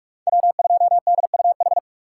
Morse code, heard as a keyed tone, about 46 wpm, 700 Hz. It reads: W2DUH
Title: USA Call Signs - 5 Characters - 6x Character Spacing - 45wpm